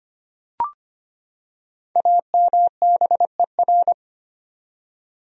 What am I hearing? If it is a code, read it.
Morse code, heard as a keyed tone, about 25 wpm, 700 Hz. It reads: AMBER
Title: USA Common Names - 25wpm